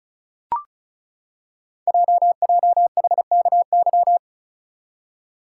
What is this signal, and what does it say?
Morse code, heard as a keyed tone, about 36 wpm, 690 Hz. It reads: JJHKY
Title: Code Groups - 5 Letters - 35wpm